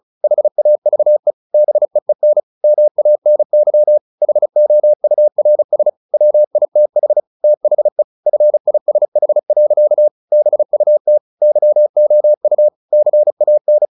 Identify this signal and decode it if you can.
Morse code, heard as a keyed tone, about 34 wpm, 600 Hz. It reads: HAVE BEEN MANY HOURS WITH THE FISH. BUT YOU CAN
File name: the old man and the sea / the old man and the sea 1 35wpm